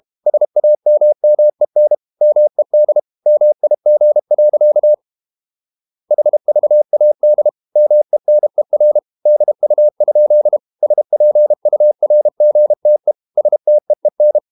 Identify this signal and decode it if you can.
Morse code, heard as a keyed tone, about 32 wpm, 600 Hz. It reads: SAMMEN MED MIG. HVAD MENER DU? SPURGTE STEEN
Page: Ferie 32 wpm